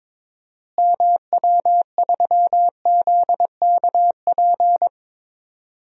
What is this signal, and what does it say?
Morse code, heard as a keyed tone, about 22 wpm, 700 Hz. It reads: MW3ZKP
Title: DXCC-100 Call Signs - 6 Characters - Repeated 2x - 22wpm